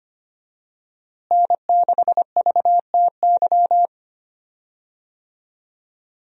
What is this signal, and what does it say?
Morse code, heard as a keyed tone, about 25 wpm, 700 Hz. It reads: N6VTY
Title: USA Call Signs - 5 Characters - Repeated 3x - 25wpm